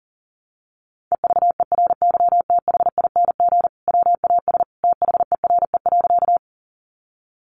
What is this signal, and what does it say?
Morse code, heard as a keyed tone, about 40 wpm, 700 Hz. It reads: EVERYTHING WAS THERE.
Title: Sentences from Top 700 Words 40wpm